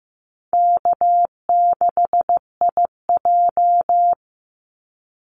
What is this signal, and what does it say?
Morse code, heard as a keyed tone, about 15 wpm, 700 Hz. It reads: K6IJ
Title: Easy Call Signs - 15wpm